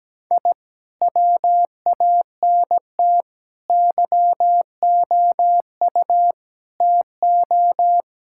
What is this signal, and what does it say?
Morse code, heard as a keyed tone, about 17 wpm, 700 Hz. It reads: I WANT YOU TO
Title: Sentences from Top 1000 Words - Review Part 5 - 17wpm